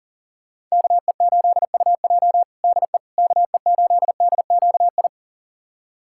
Morse code, heard as a keyed tone, about 40 wpm, 700 Hz. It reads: KE8UJ DE KE8DQI
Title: USA Callsign DE Callsign 40wpm